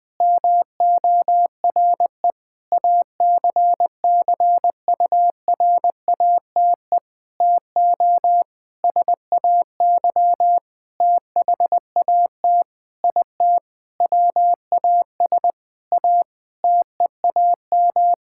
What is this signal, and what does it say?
Morse code, heard as a keyed tone, about 19 wpm, 700 Hz. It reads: MORE ACCURATE TO SAY THAT IT WAS A TEAM
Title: Sentences with 8-Word Phrases - 20wpm